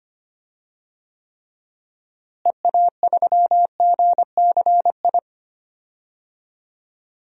Morse code, heard as a keyed tone, about 25 wpm, 700 Hz. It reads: EA3GCI
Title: DXCC-100 Call Signs - 6 Characters - Repeated 2x - 25wpm